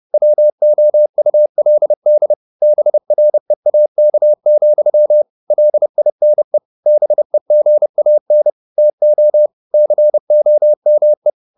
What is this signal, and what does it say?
Morse code, heard as a keyed tone, about 30 wpm, 600 Hz. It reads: WOULD BREAK, LINE BEGAN TO COME